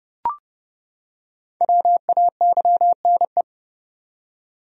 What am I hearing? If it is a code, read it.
Morse code, heard as a keyed tone, about 31 wpm, 700 Hz. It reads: WAYNE
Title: USA Common Names - Max 5 Letters - 30wpm